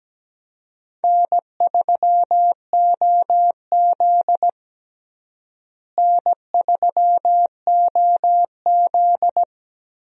Morse code, heard as a keyed tone, about 17 wpm, 700 Hz. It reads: N3OZ N3OZ
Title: USA Call Signs - 4 Characters - Repeated 2x - 17wpm